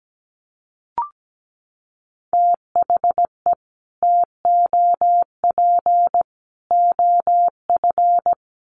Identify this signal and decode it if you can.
Morse code, heard as a keyed tone, about 17 wpm, 700 Hz. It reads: THE TOP OF
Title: Sets of 3 Words 17wpm